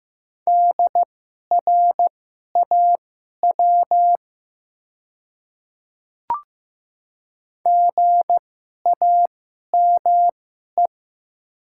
Morse code, heard as a keyed tone, about 15 wpm, spaced as for 11 wpm, 700 Hz. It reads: DRAW GAME
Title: 2x Character Spacing - Top 500 Words - 15wpm